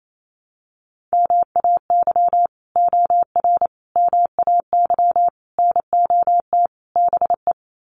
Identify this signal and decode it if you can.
Morse code, heard as a keyed tone, about 28 wpm, 700 Hz. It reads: MAY OR MAY NOT BE
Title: Sets of 5 Words 28wpm